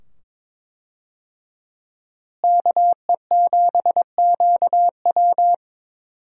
Morse code, heard as a keyed tone, about 22 wpm, 700 Hz. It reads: KE7QW
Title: USA Call Signs - 5 Characters - Repeated 2x - 22wpm